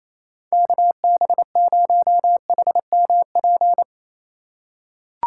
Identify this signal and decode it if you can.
Morse code, heard as a keyed tone, about 28 wpm, 700 Hz. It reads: KB0HMP
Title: USA Call Signs - 6 Characters - 2x Character Spacing - 28wpm